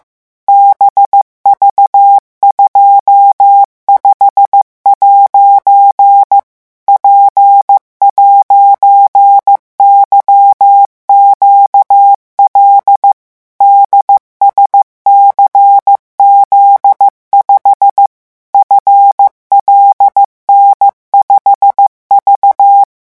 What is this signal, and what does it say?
Morse code, heard as a keyed tone, about 15 wpm, 800 Hz. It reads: BV25' P'YQL DSCZ5 FLN5V
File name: morse mp3 / revision13-vitesse 14 mots